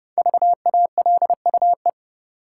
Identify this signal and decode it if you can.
Morse code, heard as a keyed tone, about 30 wpm, 700 Hz. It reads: VALUE
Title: Sentences from Top 600 Words - New Word Drill - 30wpm